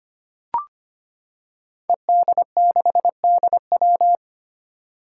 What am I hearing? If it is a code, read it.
Morse code, heard as a keyed tone, about 25 wpm, 700 Hz. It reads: ED6DW
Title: Five Challenging Letter-Number Combinations 25wpm